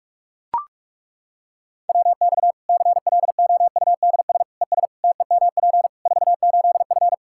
Morse code, heard as a keyed tone, about 42 wpm, 700 Hz. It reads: WX CLOUDS ES TEMP 48F
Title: Practice QSO - Exchange 2 - Info - 45wpm